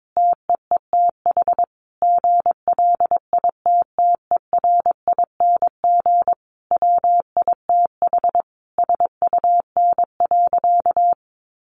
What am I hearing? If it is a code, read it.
Morse code, heard as a keyed tone, about 22 wpm, 700 Hz. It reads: TEETH GLITTERING WITH SUN.
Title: A Sound Of Thunder 22wpm